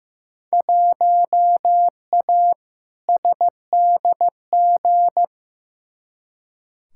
Morse code, heard as a keyed tone, about 15 wpm, 700 Hz. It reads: 1A SDG
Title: Field Day Exchanges Encore - 15wpm